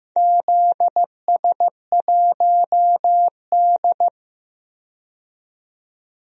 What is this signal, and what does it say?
Morse code, heard as a keyed tone, about 15 wpm, 700 Hz. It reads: ZS1D
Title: DXCC-100 Call Signs - 4 Characters - 15wpm